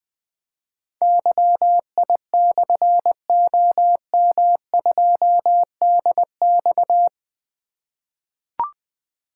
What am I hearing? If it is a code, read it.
Morse code, heard as a keyed tone, about 20 wpm, 700 Hz. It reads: YI/OM2DX
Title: Difficult Call Signs 20wpm